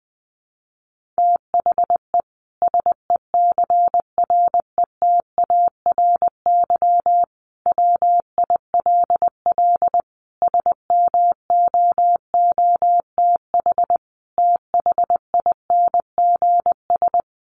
Morse code, heard as a keyed tone, about 20 wpm, 700 Hz. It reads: THE SECRETARY WILL SMOOTH THINGS
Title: Sentences from Top 1400 Words 20wpm